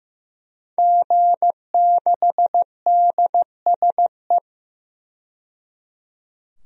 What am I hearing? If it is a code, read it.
Morse code, heard as a keyed tone, about 15 wpm, 700 Hz. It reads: G6DSE
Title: DXCC-100 Call Signs - 5 Characters - 15wpm